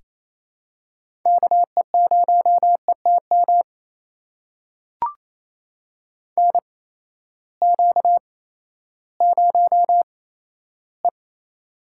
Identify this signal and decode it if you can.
Morse code, heard as a keyed tone, about 28 wpm, spaced as for 14 wpm, 700 Hz. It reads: KE0ETM NQ0E
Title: USA Call Signs - 8x Character Spacing - 28wpm